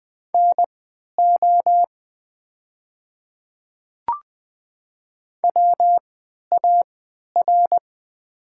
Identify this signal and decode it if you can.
Morse code, heard as a keyed tone, about 20 wpm, spaced as for 11 wpm, 700 Hz. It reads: NO WAR
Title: 3x Character Spacing - Top 500 Words - 20wpm